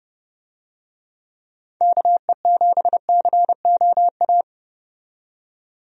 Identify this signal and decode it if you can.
Morse code, heard as a keyed tone, about 32 wpm, 700 Hz. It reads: KE7COA
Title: USA Call Signs - 6 Characters - 30wpm